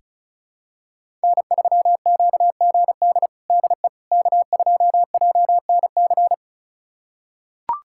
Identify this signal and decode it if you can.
Morse code, heard as a keyed tone, about 36 wpm, 700 Hz. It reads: N3QGD DE K2JNC
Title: USA Callsign DE Callsign - 5 Characters - 35wpm